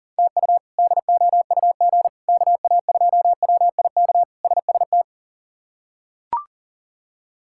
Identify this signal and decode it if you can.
Morse code, heard as a keyed tone, about 40 wpm, 700 Hz. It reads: TU DOUG KA2WIK SST
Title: K1USN SST QSO Practice - 40wpm